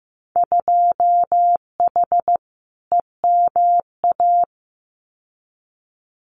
Morse code, heard as a keyed tone, about 15 wpm, 700 Hz. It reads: 2H EMA
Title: Winter Field Day Exchanges 15wpm